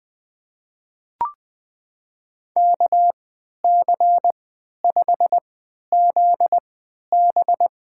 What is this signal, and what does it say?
Morse code, heard as a keyed tone, about 20 wpm, spaced as for 11 wpm, 700 Hz. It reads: KC5ZB
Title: USA Call Signs - 3x Character Spacing - 20wpm